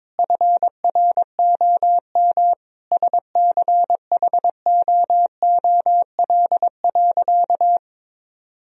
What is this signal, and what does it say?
Morse code, heard as a keyed tone, about 22 wpm, 700 Hz. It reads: FROM SCHOOL.
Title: Sentences from Top 1000 Words - Review Part 1 - 22wpm